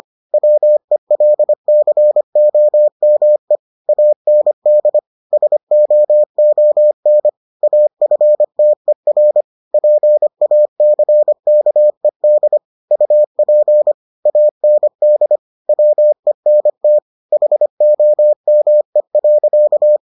Morse code, heard as a keyed tone, about 25 wpm, 600 Hz. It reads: WELCOME AND SOON AFTER PACKED UP AND WENT HOME.